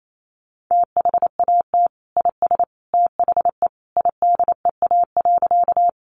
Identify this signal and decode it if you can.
Morse code, heard as a keyed tone, about 28 wpm, 700 Hz. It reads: THAT IS THE IDEA.
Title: Sentences from Top 300 Words 28wpm